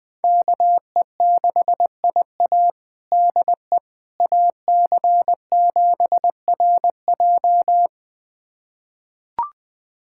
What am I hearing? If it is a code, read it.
Morse code, heard as a keyed tone, about 20 wpm, 700 Hz. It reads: KE6IA DE AC7RJ
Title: USA Callsign DE Callsign - 5 Characters - 20wpm